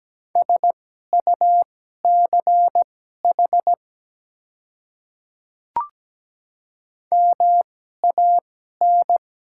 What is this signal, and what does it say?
Morse code, heard as a keyed tone, about 17 wpm, spaced as for 12 wpm, 700 Hz. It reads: SUCH MAN
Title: 2x Character Spacing - Top 300 Words - 17wpm